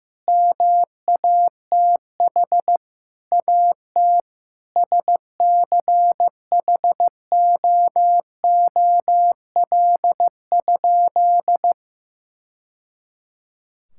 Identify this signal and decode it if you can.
Morse code, heard as a keyed tone, about 15 wpm, 700 Hz. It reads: MATH AT SCHOOL?
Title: Sentences from Top 1000 Words - Review Part 6 - 15wpm